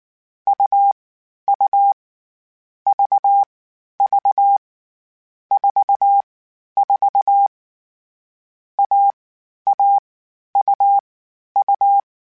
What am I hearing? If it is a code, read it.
Morse code, heard as a keyed tone, about 19 wpm, 800 Hz. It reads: U U V V 4 4 A A U U
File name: Difficult Combos / Difficult Combos-mp3s / AUV AUV4 @18